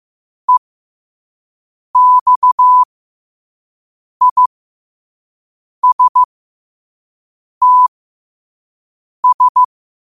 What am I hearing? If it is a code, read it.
Morse code, heard as a keyed tone, about 15 wpm, spaced as for 5 wpm, 1000 Hz. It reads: EXISTS
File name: qotdfiles / 5wpm